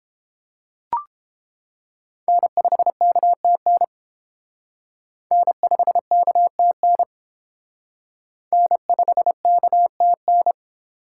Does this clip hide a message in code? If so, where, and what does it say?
Yes: Morse code from 2.3 to 10.5 s, heard as a keyed tone, about 30 wpm, 700 Hz. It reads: N5KTN N5KTN N5KTN